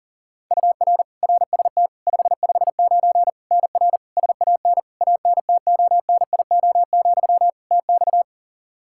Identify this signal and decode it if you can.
Morse code, heard as a keyed tone, about 40 wpm, 700 Hz. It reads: UR RST 559 NR SAN ANTONIO, TX
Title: K1USN SST QSO Practice 40wpm